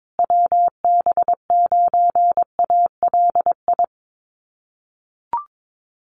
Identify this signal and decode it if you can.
Morse code, heard as a keyed tone, about 22 wpm, 700 Hz. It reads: WB9ALI
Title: USA Call Signs Repeated 3x 22wpm